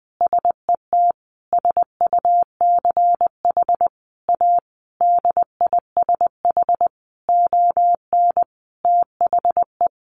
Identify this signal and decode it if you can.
Morse code, heard as a keyed tone, about 20 wpm, 700 Hz. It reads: SET SUCH A DISH ON THE
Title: Sentences from Top 1300 Words 20wpm